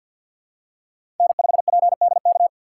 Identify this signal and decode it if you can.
Morse code, heard as a keyed tone, about 53 wpm, 700 Hz. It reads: N5PDK
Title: USA Call Signs - 5 Characters - 3x Character Spacing - 50wpm